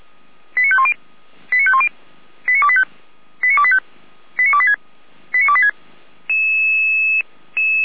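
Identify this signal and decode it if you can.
Five-tone selective call (ZVEI), heard as quick runs of five tones, tones of about 70 ms. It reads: 89510 89510 89285 89285 89286 89286
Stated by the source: ZVEI five-tone call received by SDR.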